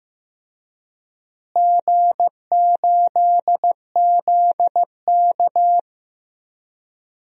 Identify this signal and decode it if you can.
Morse code, heard as a keyed tone, about 15 wpm, 700 Hz. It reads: G8ZK
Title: DXCC-100 Call Signs - 4 Characters - Repeated 2x - 15wpm